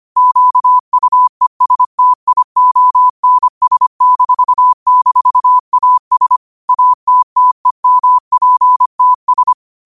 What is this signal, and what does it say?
Morse code, heard as a keyed tone, about 25 wpm, 1000 Hz. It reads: QUESTIONS--AS ATTEMPTS
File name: qotdfiles / 25wpm